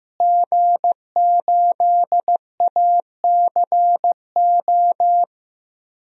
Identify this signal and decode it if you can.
Morse code, heard as a keyed tone, about 15 wpm, 700 Hz. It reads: G8ACO
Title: DXCC-100 Call Signs - 5 Characters - Repeated 3x - 15wpm